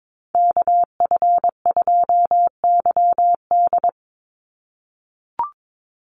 Morse code, heard as a keyed tone, about 22 wpm, 700 Hz. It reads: KF2YD
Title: USA Call Signs 22wpm